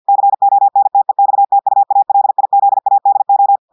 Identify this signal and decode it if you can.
Morse code, heard as a keyed tone, about 50 wpm, 800 Hz. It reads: CONTEXTUALIZADO
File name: audios / portugais / portugais texte en clair 1 vitesse 50 mots minutes